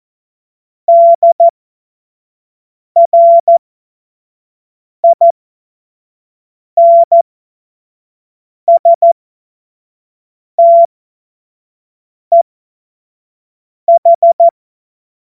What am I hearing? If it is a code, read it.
Morse code, heard as a keyed tone, about 14 wpm, spaced as for 5 wpm, 680 Hz. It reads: DRIN STEH
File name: Morsekurs / Witze / Witz 20. Juli 2021, 25 F70